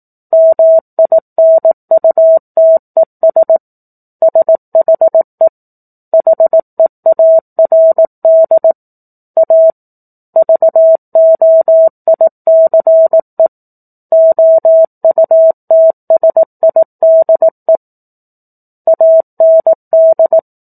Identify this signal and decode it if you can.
Morse code, heard as a keyed tone, about 18 wpm, 650 Hz. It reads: MINUTES SHE HEARD A VOICE OUTSIDE AND
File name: 04-Chapter-04 18 WPM